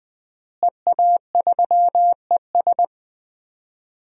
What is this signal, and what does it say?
Morse code, heard as a keyed tone, about 20 wpm, 700 Hz. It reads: EA3ES
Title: DXCC-100 Call Signs - 5 Characters - Repeated 2x - 20wpm